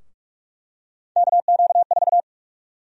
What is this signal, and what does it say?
Morse code, heard as a keyed tone, about 44 wpm, 700 Hz. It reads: KQ4
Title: USA Call Sign Prefixes - 45wpm